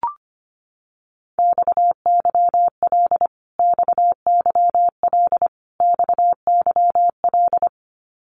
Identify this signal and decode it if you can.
Morse code, heard as a keyed tone, about 25 wpm, 700 Hz. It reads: XYL XYL XYL
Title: Abbreviations Repeated 3x - 25wpm